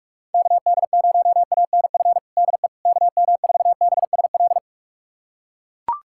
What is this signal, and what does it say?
Morse code, heard as a keyed tone, about 41 wpm, 700 Hz. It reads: KD0ANF DE KM4BSL
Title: USA Callsign DE Callsign - 6 Characters - 45wpm